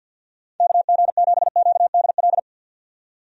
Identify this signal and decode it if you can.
Morse code, heard as a keyed tone, about 51 wpm, 700 Hz. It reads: KG7QDL
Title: USA Call Signs - 6 Characters - 50wpm